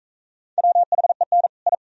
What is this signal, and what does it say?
Morse code, heard as a keyed tone, about 41 wpm, 700 Hz. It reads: WHEN I